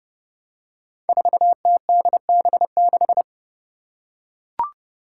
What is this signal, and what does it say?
Morse code, heard as a keyed tone, about 29 wpm, 700 Hz. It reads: 4TDB6
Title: Five Challenging Letter-Number Combinations 30wpm